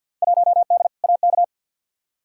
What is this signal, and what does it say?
Morse code, heard as a keyed tone, about 50 wpm, 700 Hz. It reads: 1D AK
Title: Field Day Exchanges - 50wpm